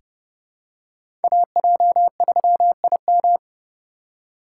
Morse code, heard as a keyed tone, about 30 wpm, 700 Hz. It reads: AJ3IM
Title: USA Call Signs - 5 Characters - 2x Character Spacing - 30wpm